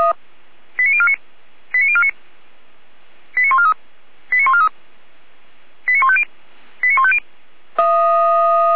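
Five-tone selective call (ZVEI), heard as quick runs of five tones, tones of about 70 ms. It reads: 89049 89049 89143 89143 89150 89150
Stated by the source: ZVEI five-tone selective call received on an SDR.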